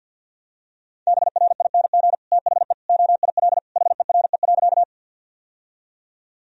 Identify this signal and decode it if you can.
Morse code, heard as a keyed tone, about 50 wpm, 700 Hz. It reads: BRING THE OIL HERE.
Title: Sentences from Top 1000 Words - Review Part 4 - 50wpm